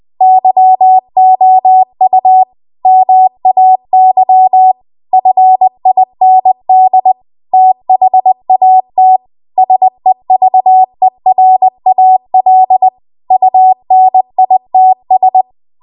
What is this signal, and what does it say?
Morse code, heard as a keyed tone, about 20 wpm, 750 Hz. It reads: YOU MAY FIND THAT SEVERAL UNITS
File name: Archive / 20 WPM / 20WPM